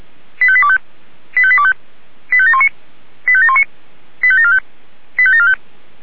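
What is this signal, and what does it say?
Five-tone selective call (ZVEI), heard as quick runs of five tones, tones of about 70 ms. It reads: 85825 85825 85719 85719 85745 85745
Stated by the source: ZVEI five-tone selective call received with an SDR.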